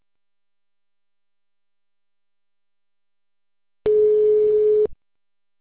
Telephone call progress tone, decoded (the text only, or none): ringback tone